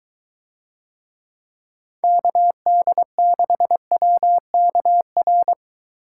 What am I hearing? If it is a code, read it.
Morse code, heard as a keyed tone, about 21 wpm, 700 Hz. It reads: KD6WKR